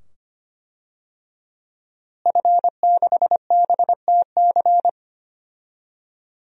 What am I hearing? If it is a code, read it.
Morse code, heard as a keyed tone, about 25 wpm, 700 Hz. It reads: F6BTC
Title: DXCC-100 Call Signs - 5 Characters - Repeated 2x - 25wpm